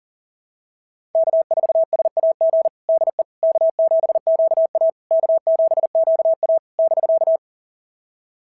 Morse code, heard as a keyed tone, about 37 wpm, 690 Hz. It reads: K4SAG DE K7QA K7QA <BK>
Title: SOTA QSO Practice 40wpm